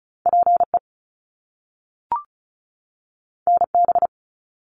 Morse code, heard as a keyed tone, about 36 wpm, 700 Hz. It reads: PE NB
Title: Canadian Provinces and Territories 35wpm